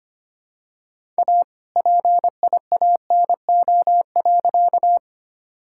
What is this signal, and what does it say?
Morse code, heard as a keyed tone, about 25 wpm, 700 Hz. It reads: A PIANO.
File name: practice / 25wpm / All-Age Appropriate Jokes - 25wpm